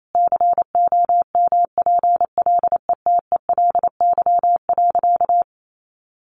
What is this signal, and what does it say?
Morse code, heard as a keyed tone, about 28 wpm, 700 Hz. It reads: COMPLETELY.
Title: Sentences from Top 1100 Words 28wpm